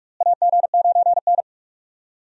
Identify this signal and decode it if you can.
Morse code, heard as a keyed tone, about 46 wpm, 700 Hz. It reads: AG9N
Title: USA Call Signs - 4 Characters - Repeated 3x - 45wpm